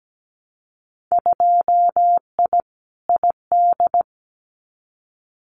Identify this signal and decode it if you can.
Morse code, heard as a keyed tone, about 17 wpm, 700 Hz. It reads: 2I ID
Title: Winter Field Day Exchanges Encore 17wpm